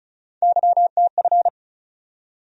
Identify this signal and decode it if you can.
Morse code, heard as a keyed tone, about 34 wpm, 700 Hz. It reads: YTF
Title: USA Call Sign Postfixes - 3 Letters 35wpm